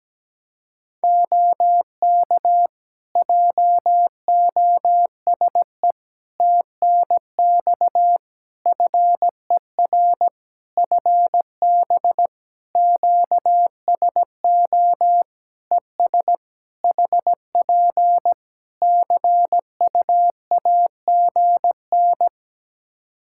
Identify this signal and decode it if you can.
Morse code, heard as a keyed tone, about 17 wpm, 700 Hz. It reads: OK JOSE TNX FER FB QSO ES HP CUAGN
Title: K1USN SST QSO Practice 17wpm